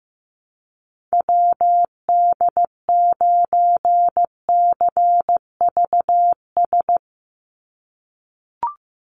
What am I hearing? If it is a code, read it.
Morse code, heard as a keyed tone, about 15 wpm, 700 Hz. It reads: WD9CVS
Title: USA Call Signs 15wpm